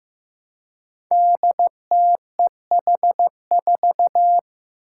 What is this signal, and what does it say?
Morse code, heard as a keyed tone, about 15 wpm, 700 Hz. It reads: DTEH4
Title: Five Challenging Letter-Number Combinations 15wpm